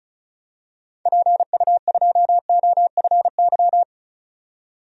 Morse code, heard as a keyed tone, about 36 wpm, 700 Hz. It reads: PU2OFY
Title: DXCC-100 Call Signs - 6 Characters - 35wpm